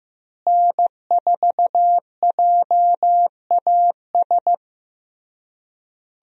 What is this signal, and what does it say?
Morse code, heard as a keyed tone, about 15 wpm, 700 Hz. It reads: N4JAS
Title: USA Call Signs - 5 Characters - Repeated 3x - 15wpm